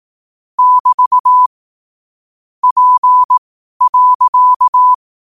Morse code, heard as a keyed tone, about 18 wpm, spaced as for 12 wpm, 1000 Hz. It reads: <BT> P.
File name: qotdfiles / 12wpm